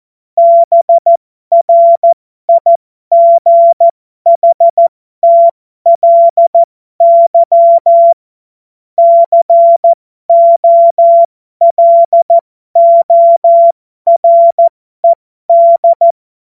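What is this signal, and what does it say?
Morse code, heard as a keyed tone, about 14 wpm, spaced as for 12 wpm, 680 Hz. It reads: BRIGHTLY COLORED
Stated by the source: Morse code joke practice.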